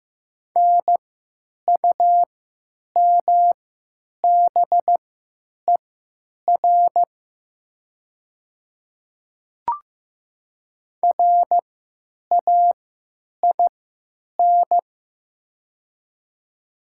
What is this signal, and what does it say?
Morse code, heard as a keyed tone, about 15 wpm, spaced as for 8 wpm, 700 Hz. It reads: NUMBER RAIN
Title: 3x Character Spacing - Top 300 Words - 15wpm